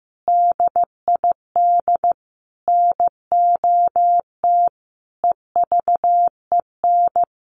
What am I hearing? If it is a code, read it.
Morse code, heard as a keyed tone, about 15 wpm, 700 Hz. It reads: DID NOT EVEN
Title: Sets of 5 Words Encore 15wpm